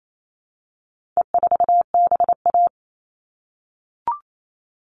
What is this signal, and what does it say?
Morse code, heard as a keyed tone, about 28 wpm, 700 Hz. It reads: E4BA
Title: All Cut Numbers - 4-Digit 28wpm